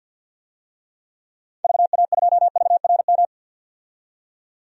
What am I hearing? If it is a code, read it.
Morse code, heard as a keyed tone, about 45 wpm, 700 Hz. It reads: VA2VRM
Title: DXCC-100 Call Signs - 6 Characters - 50wpm